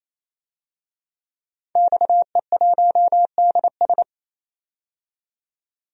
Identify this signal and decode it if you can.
Morse code, heard as a keyed tone, about 28 wpm, 700 Hz. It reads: XE1DS
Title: DXCC-100 Call Signs - 5 Characters - 28wpm